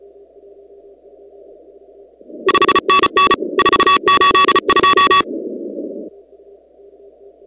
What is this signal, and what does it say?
Morse code, heard as a keyed tone, about 36 wpm, 1130 Hz. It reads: 5NN 482
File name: contests / 2019-cw-wpx-kt5j-k5pi- -2330 0-16